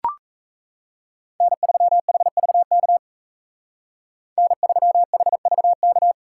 Morse code, heard as a keyed tone, about 34 wpm, 700 Hz. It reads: N3HVK N3HVK
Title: Speed Racing - USA Call Signs 28wpm